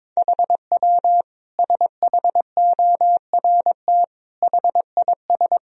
Morse code, heard as a keyed tone, about 22 wpm, 700 Hz. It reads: HW SHORT HIS
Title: Sentences from Top 400 Words and Abbreviations 22wpm